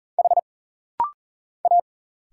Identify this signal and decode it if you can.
Morse code, heard as a keyed tone, about 40 wpm, 700 Hz. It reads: H A
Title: Single Letter-Number Alt - Rapid-Fire 40wpm